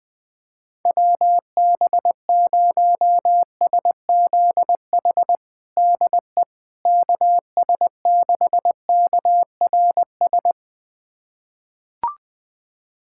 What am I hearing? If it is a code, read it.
Morse code, heard as a keyed tone, about 20 wpm, 700 Hz. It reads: WB0SZH DE KS6KRS
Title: USA Callsign DE Callsign 20wpm